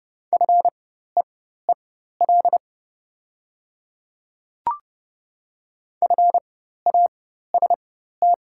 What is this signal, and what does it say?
Morse code, heard as a keyed tone, about 30 wpm, spaced as for 14 wpm, 690 Hz. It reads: FEEL FAST
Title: 4x Character Spacing - Top 500 Words - 30wpm